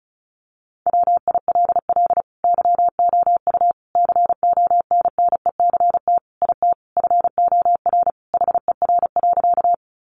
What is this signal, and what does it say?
Morse code, heard as a keyed tone, about 34 wpm, 700 Hz. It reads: WILL YOU CONNECT IT FOR HER.
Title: Sentences from Top 1500 Words 35wpm